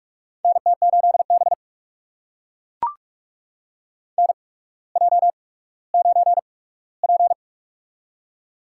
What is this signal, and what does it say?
Morse code, heard as a keyed tone, about 44 wpm, spaced as for 19 wpm, 690 Hz. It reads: NT8B NJ9P
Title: USA Call Signs - 4 Characters - 8x Character Spacing - 45wpm